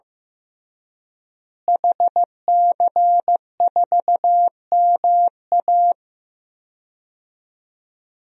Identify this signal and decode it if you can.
Morse code, heard as a keyed tone, about 15 wpm, 700 Hz. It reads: HC4MA
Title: DXCC-100 Call Signs - 5 Characters - Repeated 2x - 15wpm